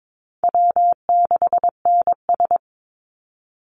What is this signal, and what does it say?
Morse code, heard as a keyed tone, about 22 wpm, 700 Hz. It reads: W6NS
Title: Easy Call Signs Repeated 2x - 22wpm